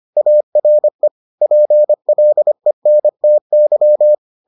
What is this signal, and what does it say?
Morse code, heard as a keyed tone, about 25 wpm, 600 Hz. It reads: ARE PLENTY